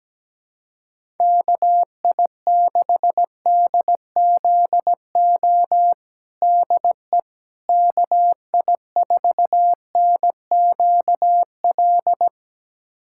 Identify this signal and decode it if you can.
Morse code, heard as a keyed tone, about 17 wpm, 700 Hz. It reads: KI6DZO DE KI4NQL
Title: USA Callsign DE Callsign 17wpm